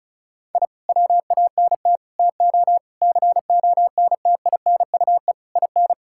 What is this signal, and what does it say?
Morse code, heard as a keyed tone, about 36 wpm, 700 Hz. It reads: I WANT TO CONTINUE IN
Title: Sentences from Top 1000 Words - Review Part 6 - 35wpm